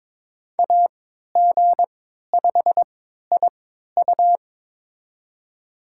Morse code, heard as a keyed tone, about 22 wpm, spaced as for 13 wpm, 700 Hz. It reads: AG5IU
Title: USA Call Signs - 5 Characters - 3x Character Spacing - 22wpm